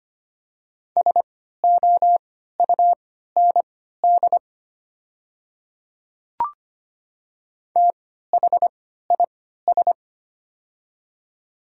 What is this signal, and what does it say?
Morse code, heard as a keyed tone, about 25 wpm, spaced as for 14 wpm, 700 Hz. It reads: SOUND THIS
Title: 3x Character Spacing - Top 200 Words - 25wpm